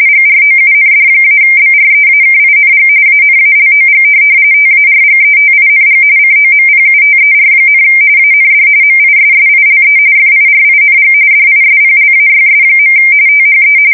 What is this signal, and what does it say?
RTTY, about 45.45 baud. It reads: GROUND-BREAKING ANALOG TARGETING COMPUTER, AND MISSILE FIRING DISPLAYS. THE
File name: fsk170 / USS-New-Jersey